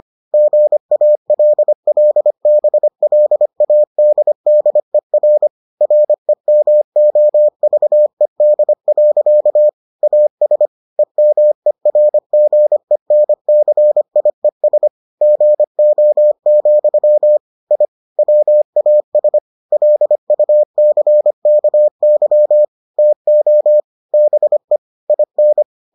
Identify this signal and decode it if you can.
Morse code, heard as a keyed tone, about 25 wpm, 600 Hz. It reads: GALLBLADDER REMOVED. AS EMERGENCIES GO, I WAS LUCKY TO BE IN